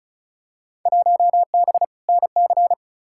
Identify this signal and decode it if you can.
Morse code, heard as a keyed tone, about 36 wpm, 700 Hz. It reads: 1B NC
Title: Field Day Exchanges - 35wpm